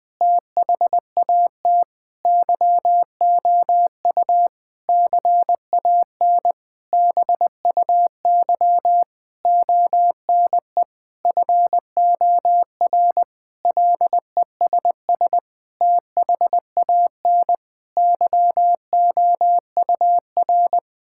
Morse code, heard as a keyed tone, about 20 wpm, 700 Hz. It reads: THAT YOU CAN BUY ONE FOR LESS THAN YOUR
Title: Sentences from Top 1000 Words - Review Part 5 - 20wpm